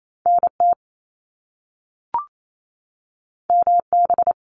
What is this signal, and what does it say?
Morse code, heard as a keyed tone, about 28 wpm, 700 Hz. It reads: NT MB
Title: Canadian Provinces and Territories 28wpm